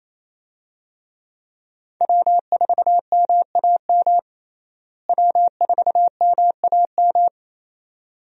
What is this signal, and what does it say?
Morse code, heard as a keyed tone, about 27 wpm, 700 Hz. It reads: W4MAM W4MAM
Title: USA Call Signs - 5 Characters - Repeated 2x - 28wpm